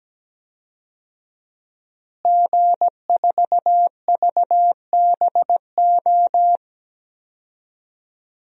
Morse code, heard as a keyed tone, about 17 wpm, 700 Hz. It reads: G4VBO
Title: DXCC-100 Call Signs - 5 Characters - Repeated 3x - 17wpm